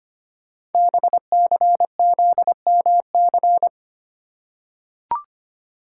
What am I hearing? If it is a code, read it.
Morse code, heard as a keyed tone, about 25 wpm, 700 Hz. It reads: BCZMC
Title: Code Groups - 5 Letters - 25wpm